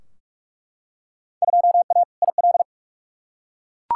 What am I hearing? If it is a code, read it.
Morse code, heard as a keyed tone, about 42 wpm, 700 Hz. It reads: 2A IL
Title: Field Day Exchanges Encore - 45wpm